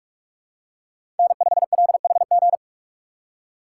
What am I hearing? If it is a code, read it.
Morse code, heard as a keyed tone, about 41 wpm, 700 Hz. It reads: N5LHG
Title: USA Call Signs - 5 Characters - 6x Character Spacing - 45wpm